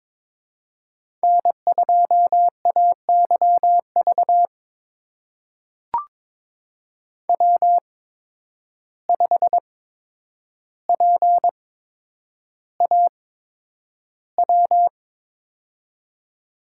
Morse code, heard as a keyed tone, about 22 wpm, spaced as for 9 wpm, 700 Hz. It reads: N2AYV W5PAW
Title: USA Call Signs - 5 Characters - 8x Character Spacing - 22wpm